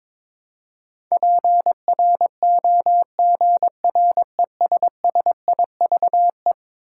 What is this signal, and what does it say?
Morse code, heard as a keyed tone, about 21 wpm, 700 Hz. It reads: PROGRESSIVE
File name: practice / 22wpm / Eleven Letter Words - 22wpm